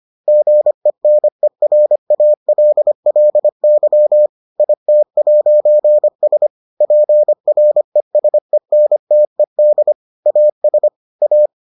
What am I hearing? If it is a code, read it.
Morse code, heard as a keyed tone, about 25 wpm, 600 Hz. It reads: GENERALLY IT'S PRESENTED AS A